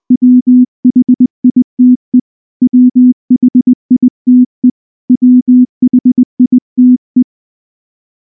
Morse code, heard as a keyed tone, about 19 wpm, 250 Hz. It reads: WHITE WHITE WHITE
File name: morse / morse-training-20WPM-3